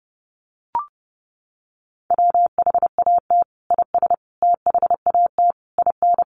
Morse code, heard as a keyed tone, about 30 wpm, 700 Hz. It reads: WHAT IS THAT IN
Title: Sentences from Top 200 Words 30wpm